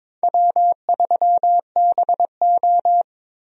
Morse code, heard as a keyed tone, about 22 wpm, 700 Hz. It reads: W3BO
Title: USA Call Signs - 4 Characters - Repeated 2x - 22wpm